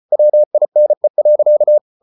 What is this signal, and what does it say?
Morse code, heard as a keyed tone, about 34 wpm, 600 Hz. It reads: WINE.